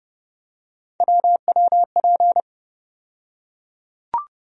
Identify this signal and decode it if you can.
Morse code, heard as a keyed tone, about 29 wpm, 700 Hz. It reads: WWP
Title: USA Call Sign Postfixes - 3 Letters 30wpm